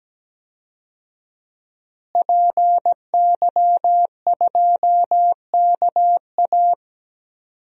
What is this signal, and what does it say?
Morse code, heard as a keyed tone, about 17 wpm, 700 Hz. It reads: PY2KA
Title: DXCC-100 Call Signs - 5 Characters - Repeated 2x - 17wpm